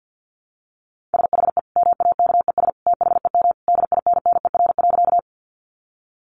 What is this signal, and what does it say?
Morse code, heard as a keyed tone, about 48 wpm, 700 Hz. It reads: SHE MAKES THEM DINNER.
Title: Easy Phrases 50wpm